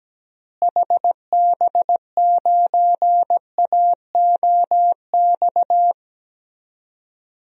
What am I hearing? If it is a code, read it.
Morse code, heard as a keyed tone, about 17 wpm, 700 Hz. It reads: HB9AOX
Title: DXCC-100 Call Signs - 6 Characters - Repeated 2x - 17wpm